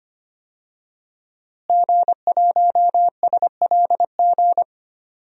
Morse code, heard as a keyed tone, about 25 wpm, 700 Hz. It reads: G1SLG
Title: DXCC-100 Call Signs - 5 Characters - Repeated 2x - 25wpm